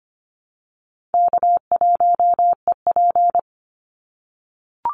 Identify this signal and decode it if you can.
Morse code, heard as a keyed tone, about 25 wpm, 700 Hz. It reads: K1EP
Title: Easy Call Signs - 25wpm